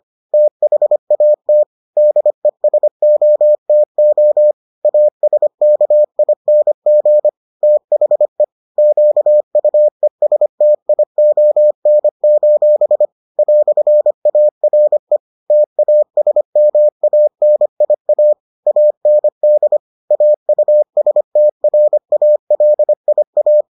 Morse code, heard as a keyed tone, about 25 wpm, 600 Hz. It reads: THAT DESOTO ASKING THE QUESTION: "ARE TASMANIA AND AUSTRALIA